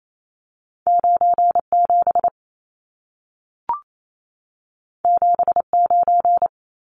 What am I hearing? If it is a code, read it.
Morse code, heard as a keyed tone, about 28 wpm, 700 Hz. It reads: 97 79
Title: Numbers - 2-Digit 28wpm